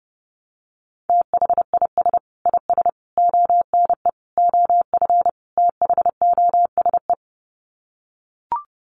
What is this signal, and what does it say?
Morse code, heard as a keyed tone, about 30 wpm, 700 Hz. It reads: THIS IS ONE OF THOSE
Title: Sets of 5 Words Encore2 30wpm